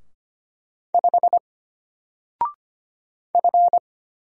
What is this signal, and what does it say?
Morse code, heard as a keyed tone, about 25 wpm, 700 Hz. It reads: 5 F
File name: Single-Letter-Number-25wpm